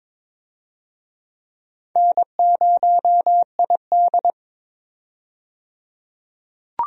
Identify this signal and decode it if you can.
Morse code, heard as a keyed tone, about 22 wpm, 700 Hz. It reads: N0ID
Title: USA Call Signs - 4 Characters - 22wpm